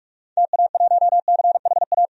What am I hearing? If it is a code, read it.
Morse code, heard as a keyed tone, about 42 wpm, 690 Hz. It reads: TA1CHA
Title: DXCC-100 Call Signs - 6 Characters - 45wpm